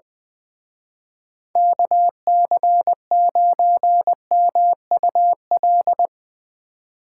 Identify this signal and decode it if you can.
Morse code, heard as a keyed tone, about 22 wpm, 700 Hz. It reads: KC9MUL
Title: Speed Racing - USA 2x3 Call Signs 20wpm